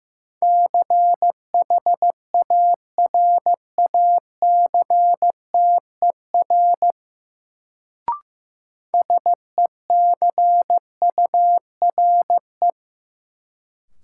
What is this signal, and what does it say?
Morse code, heard as a keyed tone, about 16 wpm, 700 Hz. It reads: CHARACTER SECURE
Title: Sentences from Top 1200 Words - New Word Drill - 15wpm